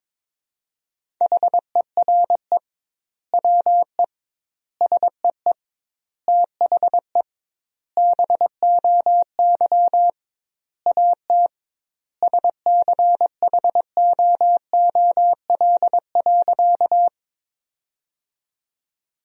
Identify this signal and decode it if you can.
Morse code, heard as a keyed tone, about 22 wpm, 700 Hz. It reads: HERE WE SEE THE BOY AT SCHOOL.
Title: 2x Word-Spacing - Sentences from Top 300 Words - 22wpm